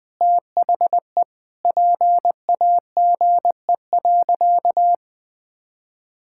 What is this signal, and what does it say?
Morse code, heard as a keyed tone, about 20 wpm, 700 Hz. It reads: THE PAGE.
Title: Sentences with 4-Word Phrases - Spoken Beginning - 20wpm